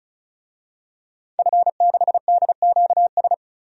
Morse code, heard as a keyed tone, about 34 wpm, 700 Hz. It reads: F6DQS
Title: DXCC-100 Call Signs - 5 Characters - 35wpm